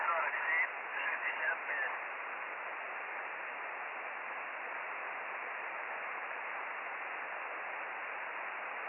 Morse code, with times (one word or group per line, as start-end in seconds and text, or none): none